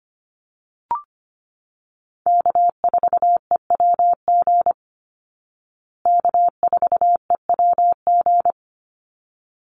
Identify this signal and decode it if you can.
Morse code, heard as a keyed tone, about 25 wpm, 700 Hz. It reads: K4EWG K4EWG
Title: Call Signs Repeated 3x - 25wpm